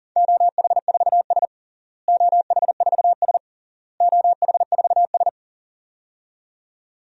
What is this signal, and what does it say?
Morse code, heard as a keyed tone, about 40 wpm, 700 Hz. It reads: OH4S OH4S OH4S
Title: DXCC-100 Call Signs - 4 Characters - Repeated 3x - 40wpm